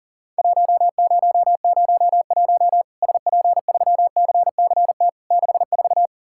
Morse code, heard as a keyed tone, about 40 wpm, 700 Hz. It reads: 1001 SP3CCT 64
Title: Worked All Europe DX Contest - QTC Exchange - 40wpm